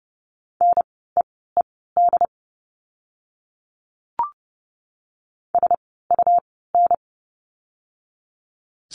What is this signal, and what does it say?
Morse code, heard as a keyed tone, about 30 wpm, spaced as for 17 wpm, 700 Hz. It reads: NEED SUN
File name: Lesson 17f - Letter u - Words 30wpm